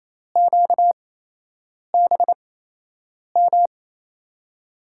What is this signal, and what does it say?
Morse code, heard as a keyed tone, about 29 wpm, spaced as for 8 wpm, 700 Hz. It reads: QBM
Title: USA Call Signs - 8x Character Spacing - 28wpm